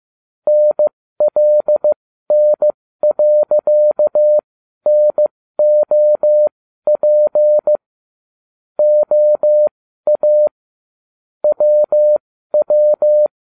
Morse code, heard as a keyed tone, about 15 wpm, spaced as for 12 wpm, 600 Hz. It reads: NLN.NOP OA WW